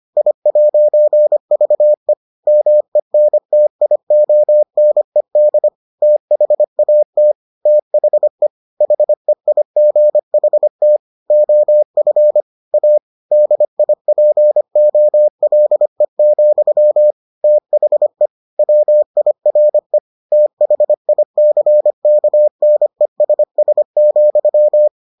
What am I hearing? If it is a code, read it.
Morse code, heard as a keyed tone, about 25 wpm, 600 Hz. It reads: I'VE MENTIONED THAT THE HEIGHT OF A DIPOLE, THE WIRE THICKNESS,